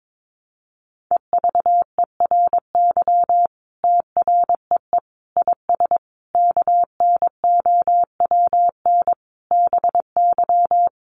Morse code, heard as a keyed tone, about 22 wpm, 700 Hz. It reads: EVERY TREE IS KNOWN BY
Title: Sentences from Top 1300 Words 22wpm